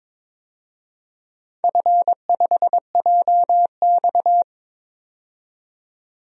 Morse code, heard as a keyed tone, about 22 wpm, 700 Hz. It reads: F5JX
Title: DXCC-100 Call Signs - 4 Characters - 22wpm